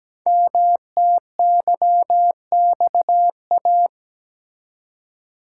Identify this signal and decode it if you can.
Morse code, heard as a keyed tone, about 17 wpm, 700 Hz. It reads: MTYXA
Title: Code Groups - 5 Letters - 17wpm